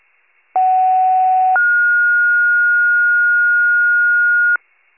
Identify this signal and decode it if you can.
Two-tone page, heard as a one-second tone, then a three-second tone, about 750 then 1430 Hz. It is heard from 0.6 to 4.6 s.